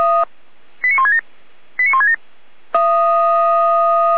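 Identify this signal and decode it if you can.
Five-tone selective call (ZVEI), heard as quick runs of five tones, tones of about 70 ms. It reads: 89167 89167
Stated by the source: Received on SDR, ZVEI five-tone call.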